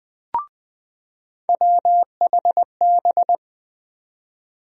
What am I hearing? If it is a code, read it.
Morse code, heard as a keyed tone, about 21 wpm, 700 Hz. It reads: WHB
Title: USA Call Sign Postfixes - 3 Letters 20wpm